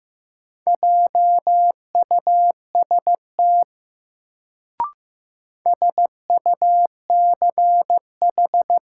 Morse code, heard as a keyed tone, about 15 wpm, 700 Hz. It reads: JUST SUCH
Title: Sentences from Top 200 Words - New Word Drill - 15wpm